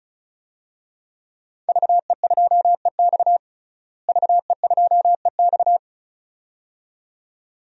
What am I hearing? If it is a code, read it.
Morse code, heard as a keyed tone, about 34 wpm, 700 Hz. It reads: VE2EX VE2EX
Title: DXCC-100 Call Signs - 5 Characters - Repeated 2x - 35wpm